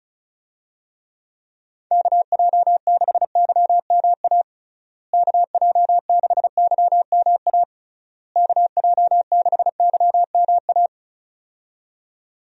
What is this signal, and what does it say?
Morse code, heard as a keyed tone, about 36 wpm, 700 Hz. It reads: KJ6YMA KJ6YMA KJ6YMA
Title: USA Call Signs - 6 Characters - Repeated 3x - 35wpm